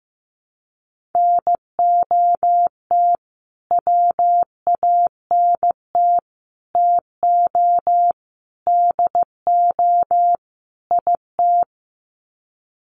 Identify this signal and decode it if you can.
Morse code, heard as a keyed tone, about 15 wpm, 700 Hz. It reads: NOT WANT TO DO IT
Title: Sets of 5 Words Encore 15wpm